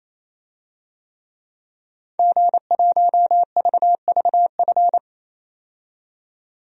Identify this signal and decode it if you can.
Morse code, heard as a keyed tone, about 28 wpm, 700 Hz. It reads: G1VVF
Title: DXCC-100 Call Signs - 5 Characters - 28wpm